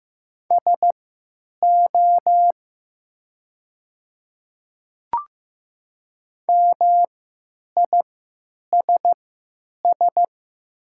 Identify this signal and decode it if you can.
Morse code, heard as a keyed tone, about 15 wpm, spaced as for 9 wpm, 700 Hz. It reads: SO MISS
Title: 3x Character Spacing - Top 500 Words - 15wpm